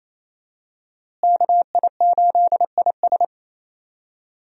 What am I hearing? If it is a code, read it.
Morse code, heard as a keyed tone, about 27 wpm, 700 Hz. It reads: KI8IS
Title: USA Call Signs - 5 Characters - Repeated 3x - 28wpm